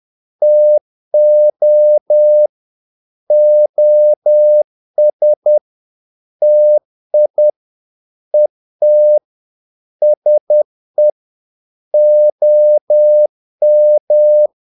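Morse code, heard as a keyed tone, about 10 wpm, 600 Hz. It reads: TO OS TI ET SE OM
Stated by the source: Morse practice of short Danish words at 10 wpm, from CW med Gnister.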